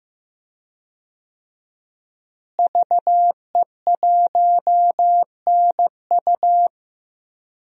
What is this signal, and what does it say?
Morse code, heard as a keyed tone, about 16 wpm, 700 Hz. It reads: VE1NU
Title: DXCC-100 Call Signs - 5 Characters - Repeated 2x - 15wpm